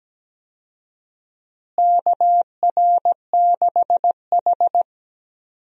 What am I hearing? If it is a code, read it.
Morse code, heard as a keyed tone, about 17 wpm, 700 Hz. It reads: KR6H
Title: USA Call Signs - 4 Characters - Repeated 2x - 17wpm